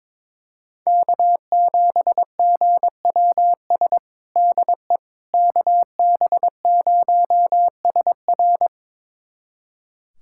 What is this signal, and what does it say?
Morse code, heard as a keyed tone, about 22 wpm, 700 Hz. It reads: K7GWS DE KB0SR
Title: USA Callsign DE Callsign - 5 Characters - 22wpm